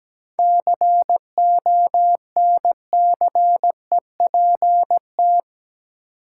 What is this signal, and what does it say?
Morse code, heard as a keyed tone, about 17 wpm, 700 Hz. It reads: CONCEPT
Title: Seven Letter Words - 17wpm